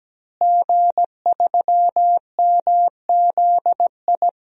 Morse code, heard as a keyed tone, about 17 wpm, 700 Hz. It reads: G3MZI
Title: DXCC-100 Call Signs - 5 Characters - Repeated 3x - 17wpm